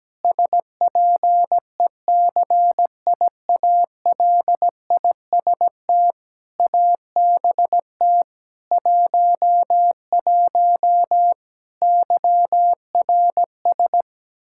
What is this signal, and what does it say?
Morse code, heard as a keyed tone, about 17 wpm, 700 Hz. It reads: SPECIALIST ABT 11 YRS
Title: Practice QSO - Exchange 3 - More Info - 17wpm